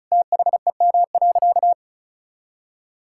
Morse code, heard as a keyed tone, about 36 wpm, 700 Hz. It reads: THEM.
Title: Sentences from Top 1000 Words - Review Part 6 - 35wpm